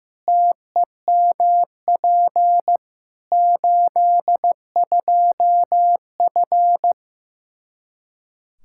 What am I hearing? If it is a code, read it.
Morse code, heard as a keyed tone, about 15 wpm, 700 Hz. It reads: TEMP 82F
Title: Practice QSO - Exchange 2 - Info - 15wpm